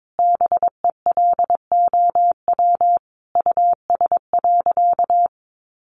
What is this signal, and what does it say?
Morse code, heard as a keyed tone, about 22 wpm, 700 Hz. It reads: BELOW US.
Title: Sentences from Top 1400 Words 22wpm